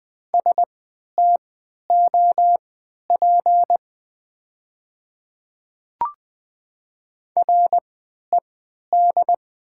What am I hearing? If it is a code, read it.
Morse code, heard as a keyed tone, about 20 wpm, spaced as for 11 wpm, 700 Hz. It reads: STOP RED
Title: 3x Character Spacing - Top 500 Words - 20wpm